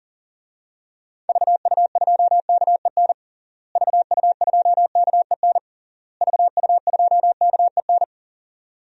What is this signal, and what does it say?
Morse code, heard as a keyed tone, about 40 wpm, 700 Hz. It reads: VU2KEN VU2KEN VU2KEN
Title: DXCC-100 Call Signs - 6 Characters - Repeated 3x - 40wpm